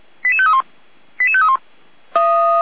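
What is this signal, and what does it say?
Five-tone selective call (ZVEI), heard as quick runs of five tones, tones of about 70 ms. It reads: 88531 88531
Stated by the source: ZVEI five-tone selective call received on an SDR.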